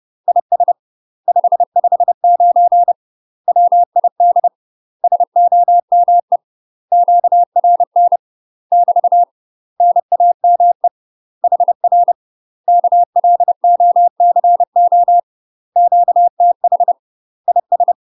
Morse code, heard as a keyed tone, about 30 wpm, 700 Hz. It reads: IS 559 WID SOME QRN <BT> NAME HR KLOCO QTH IS
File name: UK MASTER QSOs TRAINING / 30WPM / UK MASTER QSOs 0016